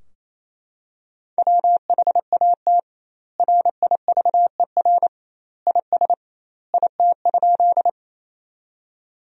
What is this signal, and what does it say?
Morse code, heard as a keyed tone, about 28 wpm, 700 Hz. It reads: WHAT RIVER IS IT?
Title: 2x Word-Spacing - Sentences from Top 300 Words - 28wpm